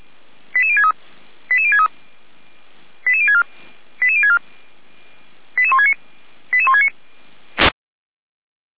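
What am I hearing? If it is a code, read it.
Five-tone selective call (ZVEI), heard as quick runs of five tones, tones of about 70 ms. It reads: 88063 88063 88064 88064 88169 88169
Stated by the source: ZVEI five-tone selective call received via SDR.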